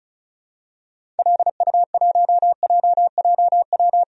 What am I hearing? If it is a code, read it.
Morse code, heard as a keyed tone, about 36 wpm, 700 Hz. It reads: LU1JJW
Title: DXCC-100 Call Signs - 6 Characters - Repeated 3x - 35wpm